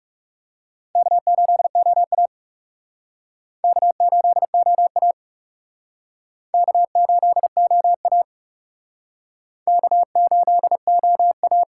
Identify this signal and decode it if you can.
Morse code, heard as a keyed tone, about 36 wpm, 700 Hz. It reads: K8OA K8OA K8OA K8OA
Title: Speed Racing - USA 1x2 Call Signs 30wpm